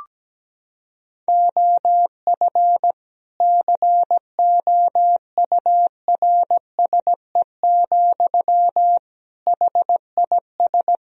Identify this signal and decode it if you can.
Morse code, heard as a keyed tone, about 17 wpm, 700 Hz. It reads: OF COURSE, HIS
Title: Sentences from Top 1000 Words - Review Part 3 - 17wpm